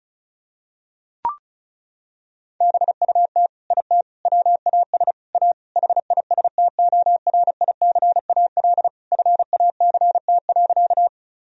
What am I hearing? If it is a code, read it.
Morse code, heard as a keyed tone, about 36 wpm, 700 Hz. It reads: BUT IT WAS A HISTORICAL FACT.
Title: Sentences from Top 1000 Words - Review Part 4 - 35wpm